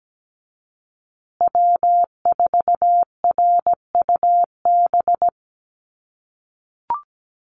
Morse code, heard as a keyed tone, about 17 wpm, 700 Hz. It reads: W4RUB
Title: USA Call Signs Repeated 2x 17wpm